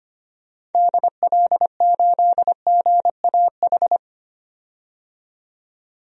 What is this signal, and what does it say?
Morse code, heard as a keyed tone, about 25 wpm, 700 Hz. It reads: DL8GAH
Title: DXCC-100 Call Signs - 6 Characters - 25wpm